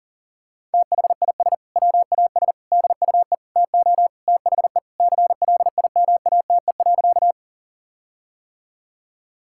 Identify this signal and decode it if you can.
Morse code, heard as a keyed tone, about 40 wpm, 700 Hz. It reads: THIS WAS DUE TO THE CLIMATE.
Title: Sentences from Top 1600 Words 40wpm